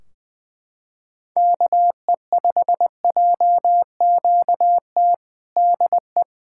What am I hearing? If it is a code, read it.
Morse code, heard as a keyed tone, about 21 wpm, 700 Hz. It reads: KE5JQT DE
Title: Practice QSO - Exchange 4 - Ending - 20wpm